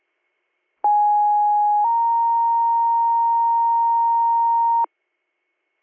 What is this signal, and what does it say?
Two-tone page, heard as a one-second tone, then a three-second tone, about 830 then 930 Hz. It is heard from 0.8 to 4.9 s.